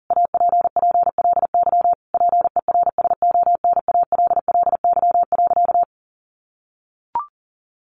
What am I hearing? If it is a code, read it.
Morse code, heard as a keyed tone, about 40 wpm, 700 Hz. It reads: APPLY PERSONALLY.
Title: Sentences from Top 1400 Words 40wpm